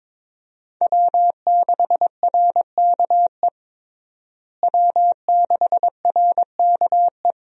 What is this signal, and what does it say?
Morse code, heard as a keyed tone, about 22 wpm, 700 Hz. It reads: W6RKE W6RKE
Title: USA Call Signs - 5 Characters - Repeated 3x - 22wpm